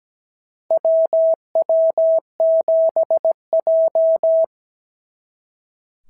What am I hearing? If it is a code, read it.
Morse code, heard as a keyed tone, about 17 wpm, 650 Hz. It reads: WW7J
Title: SOTA QSO Practice 17wpm